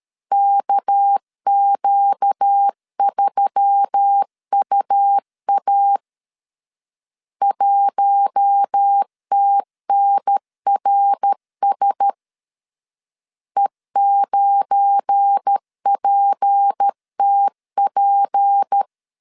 Morse code, heard as a keyed tone, about 13 wpm, 800 Hz. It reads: KQ3UA 1TNRS E9PTP
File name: LEZIONI / 8 YJ45 / 8 YJ 12wpm 2235